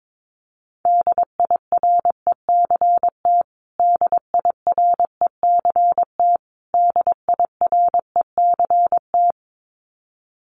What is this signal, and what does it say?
Morse code, heard as a keyed tone, about 22 wpm, 700 Hz. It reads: DIRECT DIRECT DIRECT
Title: Top 500 Words Repeated 22wpm